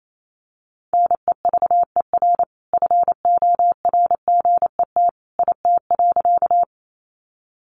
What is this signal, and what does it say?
Morse code, heard as a keyed tone, about 28 wpm, 700 Hz. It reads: NEVER FORGET IT.
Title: Sentences from Top 1200 Words 28wpm